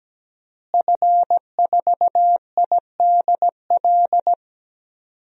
Morse code, heard as a keyed tone, about 17 wpm, 700 Hz. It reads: F4IDL
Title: DXCC-100 Call Signs - 5 Characters - Repeated 2x - 17wpm